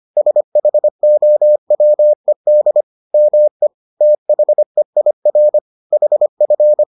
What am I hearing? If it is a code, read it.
Morse code, heard as a keyed tone, about 25 wpm, 600 Hz. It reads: SHOWED ME THEIR HF